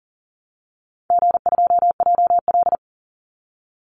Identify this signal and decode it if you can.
Morse code, heard as a keyed tone, about 40 wpm, 700 Hz. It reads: G2JL
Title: Call Signs - 40wpm